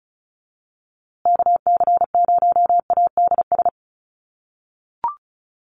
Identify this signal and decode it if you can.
Morse code, heard as a keyed tone, about 34 wpm, 700 Hz. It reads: KC0ADS
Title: USA Call Signs Repeated 2x 35wpm